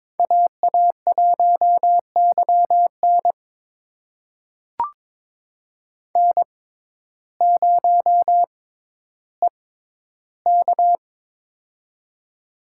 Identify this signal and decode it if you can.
Morse code, heard as a keyed tone, about 21 wpm, spaced as for 12 wpm, 700 Hz. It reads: AA1YN N0EK
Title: USA Call Signs - 6x Character Spacing - 22wpm